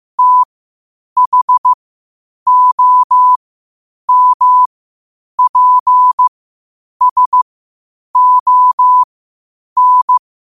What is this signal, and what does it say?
Morse code, heard as a keyed tone, about 15 wpm, spaced as for 8 wpm, 1000 Hz. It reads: THOMPSON
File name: qotdfiles / 8wpm